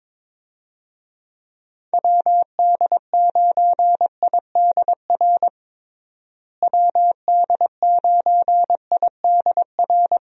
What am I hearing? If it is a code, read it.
Morse code, heard as a keyed tone, about 22 wpm, 700 Hz. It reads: WD9IDR WD9IDR
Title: USA Call Signs - 6 Characters - Repeated 2x - 22wpm